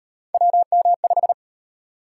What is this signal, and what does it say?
Morse code, heard as a keyed tone, about 36 wpm, 700 Hz. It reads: WM5